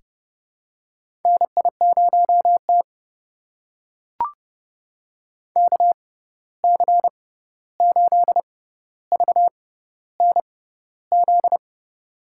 Morse code, heard as a keyed tone, about 30 wpm, spaced as for 14 wpm, 700 Hz. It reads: NI0T KC8VNZ
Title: USA Call Signs - 6x Character Spacing - 30wpm